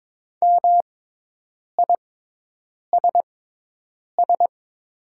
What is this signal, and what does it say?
Morse code, heard as a keyed tone, about 22 wpm, spaced as for 8 wpm, 700 Hz. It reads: MISS
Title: 6x Character Spacing - Top 500 Words - 22wpm